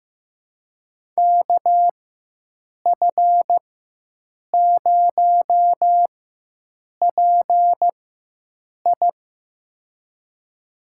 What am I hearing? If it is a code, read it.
Morse code, heard as a keyed tone, about 15 wpm, spaced as for 7 wpm, 700 Hz. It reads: KF0PI
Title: USA Call Signs - 5 Characters - 4x Character Spacing - 15wpm